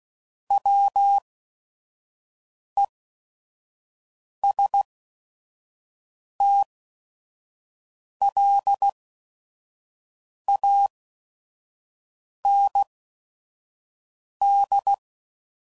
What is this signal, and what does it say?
Morse code, heard as a keyed tone, about 16 wpm, spaced as for 5 wpm, 800 Hz. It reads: WESTLAND
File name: CW Code Groups @5WPM / fromSM's / Cities